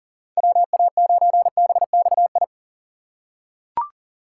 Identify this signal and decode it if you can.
Morse code, heard as a keyed tone, about 40 wpm, 700 Hz. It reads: WA9BXI
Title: USA Call Signs - 2x Character Spacing - 40wpm